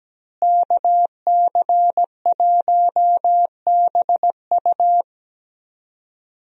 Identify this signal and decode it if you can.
Morse code, heard as a keyed tone, about 17 wpm, 700 Hz. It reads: KC1BU
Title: USA Call Signs - 5 Characters - Repeated 3x - 17wpm